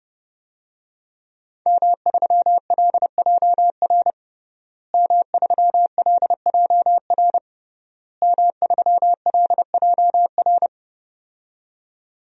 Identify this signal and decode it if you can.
Morse code, heard as a keyed tone, about 30 wpm, 700 Hz. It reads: M3LJR M3LJR M3LJR
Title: DXCC-100 Call Signs - 5 Characters - Repeated 3x - 30wpm